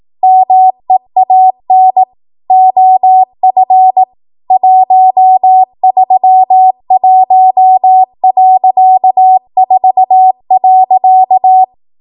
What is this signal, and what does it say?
Morse code, heard as a keyed tone, about 18 wpm, 750 Hz. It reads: MEAN OF 131.4.
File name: Morse / 18WPM